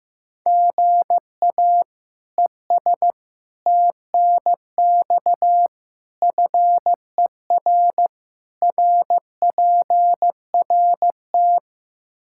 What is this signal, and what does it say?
Morse code, heard as a keyed tone, about 15 wpm, 700 Hz. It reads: GA ES TNX FER RPRT
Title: K1USN SST QSO Practice 15wpm